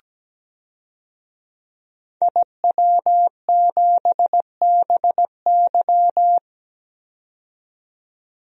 Morse code, heard as a keyed tone, about 17 wpm, 700 Hz. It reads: IW7BY
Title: DXCC-100 Call Signs - 5 Characters - 17wpm